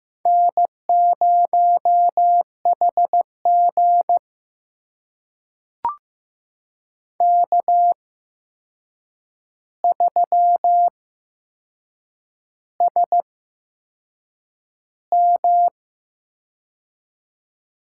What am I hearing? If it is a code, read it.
Morse code, heard as a keyed tone, about 15 wpm, spaced as for 6 wpm, 700 Hz. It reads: N0HG K3SM
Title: USA Call Signs - 4 Characters - 8x Character Spacing - 15wpm